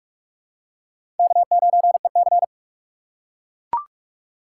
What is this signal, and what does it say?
Morse code, heard as a keyed tone, about 42 wpm, 700 Hz. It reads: K9EC
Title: USA Call Signs - 4 Characters - 3x Character Spacing - 45wpm